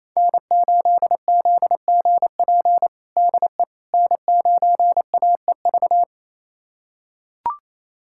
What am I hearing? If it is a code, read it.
Morse code, heard as a keyed tone, about 28 wpm, 700 Hz. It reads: N8ZGP DE N9AEV
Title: USA Callsign DE Callsign - 5 Characters - 28wpm